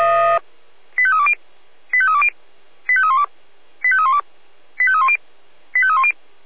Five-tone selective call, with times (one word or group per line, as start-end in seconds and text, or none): 1.0-1.3 85320
1.9-2.3 85320
2.9-3.2 85312
3.8-4.2 85312
4.8-5.2 85310
5.7-6.1 85310